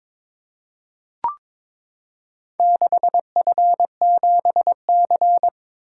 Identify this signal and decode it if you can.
Morse code, heard as a keyed tone, about 22 wpm, 700 Hz. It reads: 6F7C
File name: Four Letter-Number Combinations - 22wpm